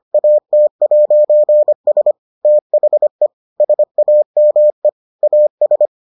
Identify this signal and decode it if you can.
Morse code, heard as a keyed tone, about 25 wpm, 600 Hz. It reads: AT'S THE SAME AS